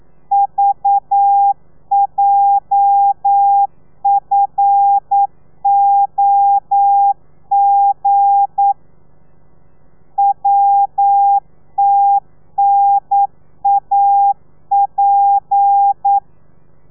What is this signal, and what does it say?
Morse code, heard as a keyed tone, about 9 wpm, 800 Hz. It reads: VJFOG WTNAP
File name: telegrafia / i2tfj / Lez8 40